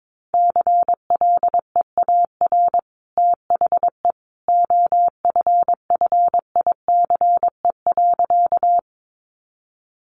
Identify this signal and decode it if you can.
Morse code, heard as a keyed tone, about 22 wpm, 700 Hz. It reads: CLEAR THE OFFICE.
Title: Sentences from Top 1000 Words 22wpm